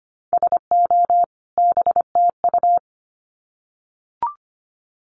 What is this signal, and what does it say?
Morse code, heard as a keyed tone, about 25 wpm, 700 Hz. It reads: SO BTU
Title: Common QSO Phrases 25wpm